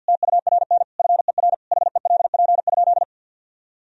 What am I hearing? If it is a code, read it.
Morse code, heard as a keyed tone, about 52 wpm, 700 Hz. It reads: TURN FER HELP?
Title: Sentences from Top 200 Words and Abbreviations 50wpm